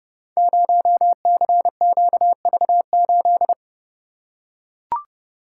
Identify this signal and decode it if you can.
Morse code, heard as a keyed tone, about 28 wpm, 700 Hz. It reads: 0CQV8
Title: Code Groups - 5 Letter:Numbers - 30wpm